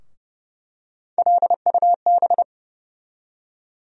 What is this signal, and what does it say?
Morse code, heard as a keyed tone, about 30 wpm, 700 Hz. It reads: LUB
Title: USA Call Sign Postfixes - 3 Letters 30wpm